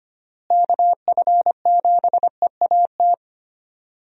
Morse code, heard as a keyed tone, about 25 wpm, 700 Hz. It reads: KF7EAT
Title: USA Call Signs - 6 Characters - Repeated 3x - 25wpm